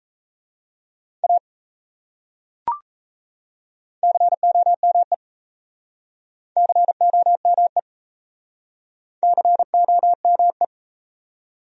Morse code, heard as a keyed tone, about 38 wpm, 700 Hz. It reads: A COME COME COME